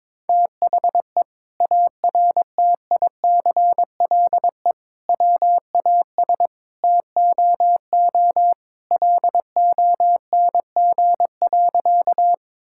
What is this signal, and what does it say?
Morse code, heard as a keyed tone, about 21 wpm, 700 Hz. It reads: THE ARTICLE WAS TOO LONG.
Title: Sentences from Top 1000 Words - Review Part 5 - 22wpm